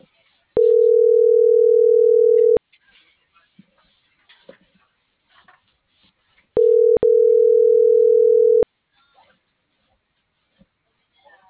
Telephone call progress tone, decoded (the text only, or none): ringback tone